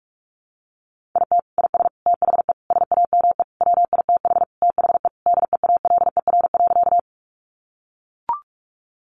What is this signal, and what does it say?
Morse code, heard as a keyed tone, about 44 wpm, 700 Hz. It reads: IT IS THE SAME WITH THE DEALER.
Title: Sentences from Top 1100 Words 45wpm